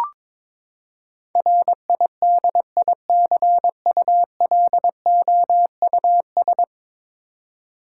Morse code, heard as a keyed tone, about 22 wpm, 700 Hz. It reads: RIDICULOUS
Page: morse-code-ninja-practice-22wpm